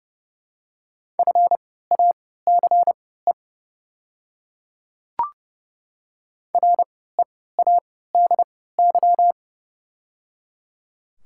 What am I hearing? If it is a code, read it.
Morse code, heard as a keyed tone, about 30 wpm, spaced as for 17 wpm, 700 Hz. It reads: FACE READY
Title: 3x Character Spacing - Top 500 Words - 30wpm